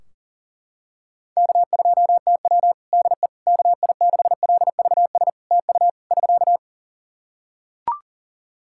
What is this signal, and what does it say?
Morse code, heard as a keyed tone, about 40 wpm, 700 Hz. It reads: K2TW DE KI6LVS TU <SK>
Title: Practice QSO - Exchange 4 - Ending - 40wpm